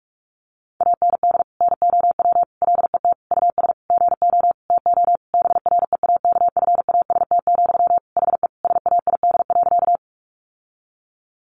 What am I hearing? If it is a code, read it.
Morse code, heard as a keyed tone, about 44 wpm, 700 Hz. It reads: AND NOW LET US GO TO BREAKFAST, HE SAID.
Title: Sentences from Top 1500 Words 45wpm